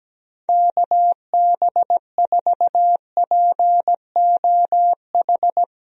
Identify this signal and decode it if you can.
Morse code, heard as a keyed tone, about 17 wpm, 700 Hz. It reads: KB4POH
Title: USA Call Signs - 6 Characters - Repeated 3x - 17wpm